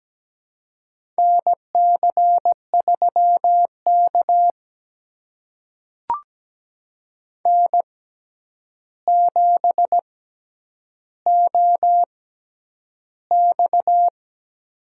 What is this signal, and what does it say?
Morse code, heard as a keyed tone, about 17 wpm, spaced as for 9 wpm, 700 Hz. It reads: NC3K N7OX
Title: USA Call Signs - 4 Characters - 6x Character Spacing - 17wpm